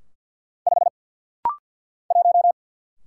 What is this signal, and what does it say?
Morse code, heard as a keyed tone, about 49 wpm, 700 Hz. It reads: 5 1
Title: Single Letter-Number - Rapid-Fire 50wpm